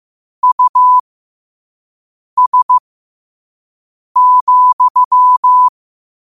Morse code, heard as a keyed tone, about 15 wpm, spaced as for 5 wpm, 1000 Hz. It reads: US,
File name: qotdfiles / 5wpm